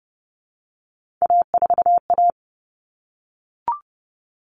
Morse code, heard as a keyed tone, about 30 wpm, 700 Hz. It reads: A4A
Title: All Cut Numbers - 3-Digit 30wpm